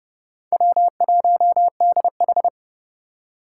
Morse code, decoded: W1DH

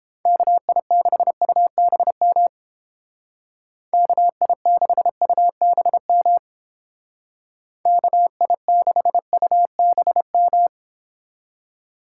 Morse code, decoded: KI6UBM KI6UBM KI6UBM

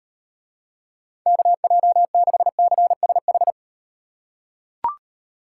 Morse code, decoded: KJ6CSH

